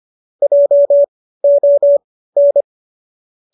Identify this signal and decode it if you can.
Morse code, heard as a keyed tone, about 25 wpm, spaced as for 15 wpm, 590 Hz. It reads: JON